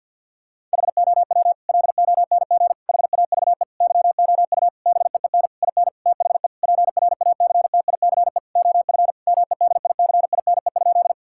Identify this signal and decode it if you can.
Morse code, heard as a keyed tone, about 50 wpm, 700 Hz. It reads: HOW LONG HAVE YOU BEEN IN THE PRACTICE OF MEDICINE?
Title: Sentences from Top 1000 Words - Review Part 6 - 50wpm